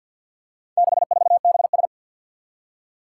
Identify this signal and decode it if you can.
Morse code, heard as a keyed tone, about 50 wpm, 700 Hz. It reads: 64BS
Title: Four Challenging Letter-Number Combinations 50wpm